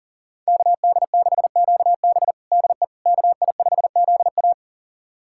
Morse code, heard as a keyed tone, about 37 wpm, 700 Hz. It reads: KD6QB DE KI5ZA
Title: USA Callsign DE Callsign - 5 Characters - 40wpm